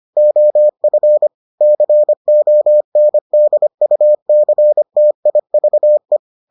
Morse code, heard as a keyed tone, about 25 wpm, 600 Hz. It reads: OF CONDUCTIVE